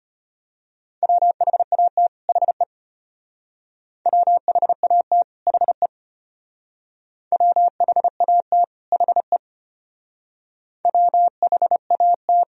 Morse code, heard as a keyed tone, about 33 wpm, 700 Hz. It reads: WHAT HE WHAT HE WHAT HE WHAT